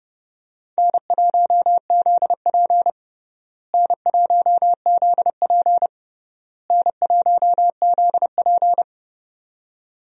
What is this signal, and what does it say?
Morse code, heard as a keyed tone, about 29 wpm, 700 Hz. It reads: N1ZP N1ZP N1ZP
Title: USA Call Signs - 4 Characters - Repeated 3x - 30wpm